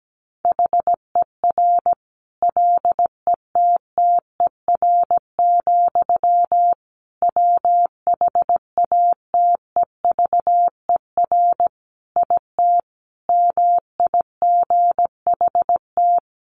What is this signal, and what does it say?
Morse code, heard as a keyed tone, about 17 wpm, 700 Hz. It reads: HER LETTER, WHATEVER IT MIGHT
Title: Sentences from Top 1100 Words 17wpm